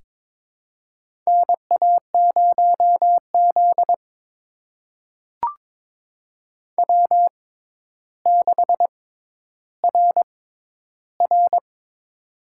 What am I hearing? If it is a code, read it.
Morse code, heard as a keyed tone, about 22 wpm, spaced as for 11 wpm, 700 Hz. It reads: NA0Z W6RR
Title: USA Call Signs - 4 Characters - 6x Character Spacing - 22wpm